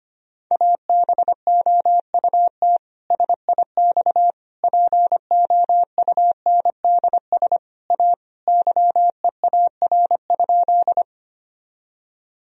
Morse code, heard as a keyed tone, about 25 wpm, 700 Hz. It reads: ABOUT SIX POUNDS A YEAR?
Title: Sentences from Top 1000 Words - Review Part 2 - 25wpm